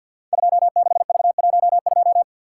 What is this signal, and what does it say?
Morse code, heard as a keyed tone, about 47 wpm, 700 Hz. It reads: 26V12
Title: Code Groups - 5 Letter:Numbers - 50wpm